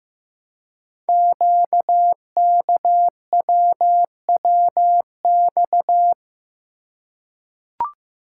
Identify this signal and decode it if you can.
Morse code, heard as a keyed tone, about 15 wpm, 700 Hz. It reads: QKWWX
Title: Code Groups - 5 Letters - 15wpm